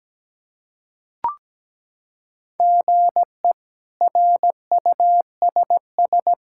Morse code, heard as a keyed tone, about 17 wpm, 700 Hz. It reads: GE RUSS
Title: K1USN SST QSO Practice - 17wpm